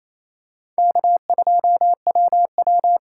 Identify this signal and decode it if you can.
Morse code, heard as a keyed tone, about 28 wpm, 700 Hz. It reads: K2WW
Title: USA Call Signs - 4 Characters - 2x Character Spacing - 28wpm